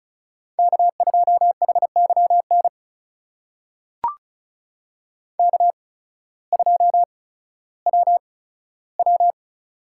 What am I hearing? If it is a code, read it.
Morse code, heard as a keyed tone, about 34 wpm, spaced as for 16 wpm, 700 Hz. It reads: K2HYN K2WW
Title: USA Call Signs - 8x Character Spacing - 35wpm